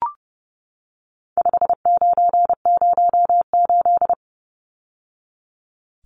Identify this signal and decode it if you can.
Morse code, heard as a keyed tone, about 30 wpm, 700 Hz. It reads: 5908
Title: Numbers - 4-Digit 30wpm